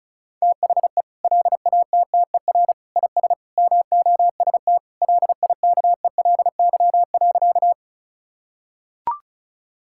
Morse code, heard as a keyed tone, about 36 wpm, 700 Hz. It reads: THE LATTER IS MOST LIKELY.
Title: Sentences from Top 1600 Words 35wpm